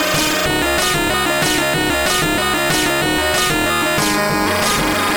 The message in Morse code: none